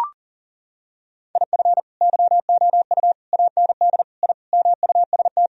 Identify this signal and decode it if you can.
Morse code, heard as a keyed tone, about 40 wpm, 700 Hz. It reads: IF YOU AND I MUST
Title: Sentences from Top 1000 Words - Review Part 1 - 40wpm